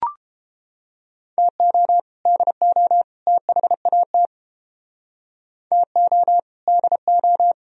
Morse code, heard as a keyed tone, about 29 wpm, 700 Hz. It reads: TO DO THAT TO DO